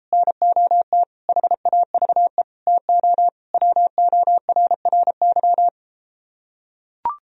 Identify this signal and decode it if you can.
Morse code, heard as a keyed tone, about 34 wpm, 700 Hz. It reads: NOT HAVE TO WORRY